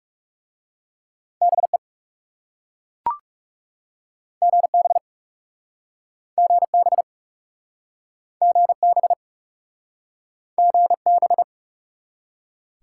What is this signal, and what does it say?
Morse code, heard as a keyed tone, about 37 wpm, 700 Hz. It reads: BE GB GB GB GB
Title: Speed-Racing Top 180 QSO Elements 30wpm